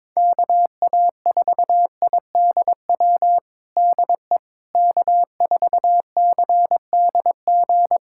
Morse code, heard as a keyed tone, about 22 wpm, 700 Hz. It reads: KA4IDW DE K4CDG
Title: USA Callsign DE Callsign 22wpm